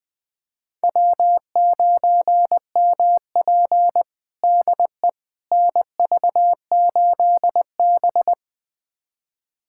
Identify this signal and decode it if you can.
Morse code, heard as a keyed tone, about 20 wpm, 700 Hz. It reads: W9MP DE NV8B
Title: USA Callsign DE Callsign - 4 Characters - 20wpm